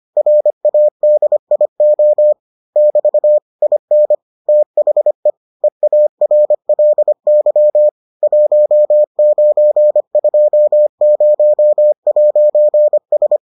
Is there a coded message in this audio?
Yes: Morse code from 0.2 to 13.4 s, heard as a keyed tone, about 25 wpm, 600 Hz. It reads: RADIO <BT> IN THE EARLY 1920'S